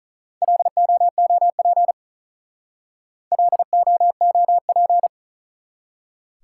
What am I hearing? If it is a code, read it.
Morse code, heard as a keyed tone, about 46 wpm, 700 Hz. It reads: LOOP LOOP